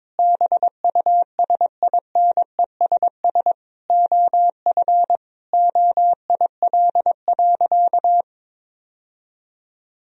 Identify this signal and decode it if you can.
Morse code, heard as a keyed tone, about 22 wpm, 700 Hz. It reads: BUSINESS OF OIL.
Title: Sentences from Top 1000 Words - Review Part 4 - 22wpm